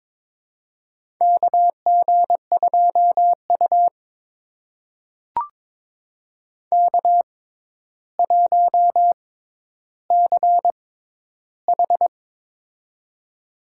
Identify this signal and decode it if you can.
Morse code, heard as a keyed tone, about 22 wpm, spaced as for 11 wpm, 700 Hz. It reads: KG2U K1CH
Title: USA Call Signs - 4 Characters - 6x Character Spacing - 22wpm